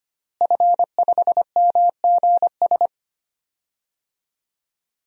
Morse code, heard as a keyed tone, about 25 wpm, 700 Hz. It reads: F5MGS
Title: DXCC-100 Call Signs - 5 Characters - Repeated 3x - 25wpm